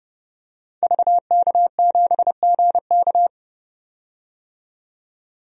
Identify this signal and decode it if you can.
Morse code, heard as a keyed tone, about 30 wpm, 700 Hz. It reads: VK7GK
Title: DXCC-100 Call Signs - 5 Characters - 30wpm